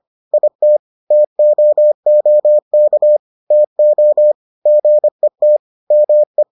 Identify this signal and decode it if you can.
Morse code, heard as a keyed tone, about 25 wpm, 600 Hz. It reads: IT TOOK TO GET ME